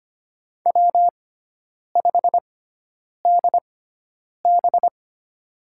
Morse code, heard as a keyed tone, about 25 wpm, spaced as for 9 wpm, 700 Hz. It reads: W5DB
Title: USA Call Signs - 4 Characters - 6x Character Spacing - 25wpm